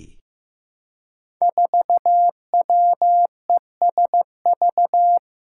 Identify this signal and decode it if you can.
Morse code, heard as a keyed tone, about 15 wpm, 700 Hz. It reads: 4WESV
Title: Five Challenging Letter-Number Combinations 15wpm